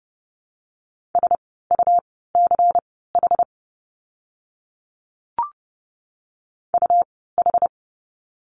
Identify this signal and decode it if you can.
Morse code, heard as a keyed tone, about 30 wpm, spaced as for 17 wpm, 700 Hz. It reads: SUCH UH
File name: Lesson 17f - Letter u - Words 30wpm